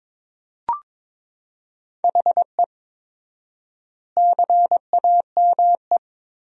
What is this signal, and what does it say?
Morse code, heard as a keyed tone, about 22 wpm, 700 Hz. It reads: HE CAME
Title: 4x Word-Spacing - Sentences from Top 300 Words - 22wpm